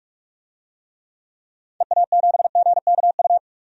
Morse code, heard as a keyed tone, about 44 wpm, 700 Hz. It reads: EA7GKU
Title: DXCC-100 Call Signs - 6 Characters - 45wpm